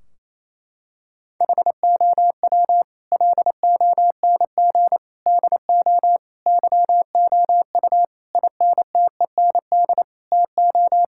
Morse code, heard as a keyed tone, about 28 wpm, 700 Hz. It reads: HOW LONG DO YOU INTEND TO
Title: Sentences from Top 1600 Words 28wpm